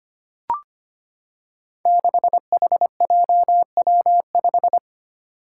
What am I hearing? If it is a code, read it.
Morse code, heard as a keyed tone, about 25 wpm, 700 Hz. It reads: 6HJW5
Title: Five Challenging Letter-Number Combinations 25wpm